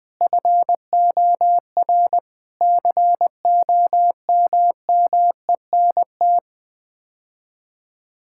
Morse code, heard as a keyed tone, about 20 wpm, 700 Hz. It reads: FOR COMMENT
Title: Sets of 6 Words - 20wpm